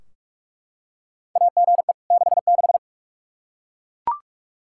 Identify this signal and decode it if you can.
Morse code, heard as a keyed tone, about 41 wpm, 700 Hz. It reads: AGE 66
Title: K1USN SST QSO Practice 45wpm